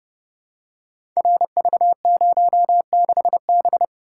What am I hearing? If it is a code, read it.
Morse code, heard as a keyed tone, about 30 wpm, 700 Hz. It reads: RV06B
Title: Code Groups - 5 Letter:Numbers - 30wpm